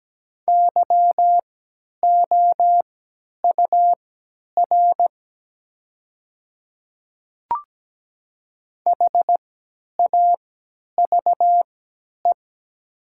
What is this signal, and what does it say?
Morse code, heard as a keyed tone, about 17 wpm, spaced as for 10 wpm, 700 Hz. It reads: YOUR HAVE
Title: 3x Character Spacing - Top 100 Words - 17wpm